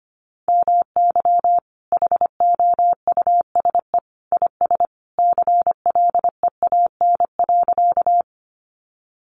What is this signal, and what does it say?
Morse code, heard as a keyed tone, about 25 wpm, 700 Hz. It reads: MY HOUSE IS CLEAN.
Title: Sentences from Top 1100 Words 25wpm